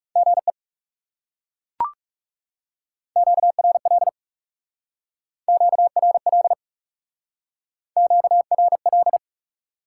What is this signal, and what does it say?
Morse code, heard as a keyed tone, about 37 wpm, 700 Hz. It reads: GE QRL QRL QRL
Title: Speed-Racing Top 90 QSO Elements 30wpm